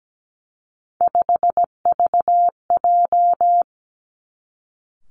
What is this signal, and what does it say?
Morse code, heard as a keyed tone, about 17 wpm, 700 Hz. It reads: 5VJ
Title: Three Challenging Letter-Number Combinations 17wpm